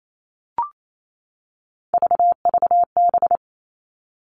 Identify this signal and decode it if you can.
Morse code, heard as a keyed tone, about 28 wpm, 700 Hz. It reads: VVB
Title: Three Challenging Letter-Number Combinations 28wpm